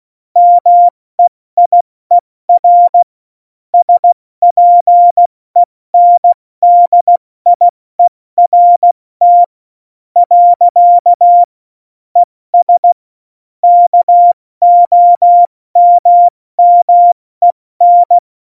Morse code, heard as a keyed tone, about 16 wpm, spaced as for 14 wpm, 700 Hz. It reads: MEIER SPENDIERT. ES KOMMEN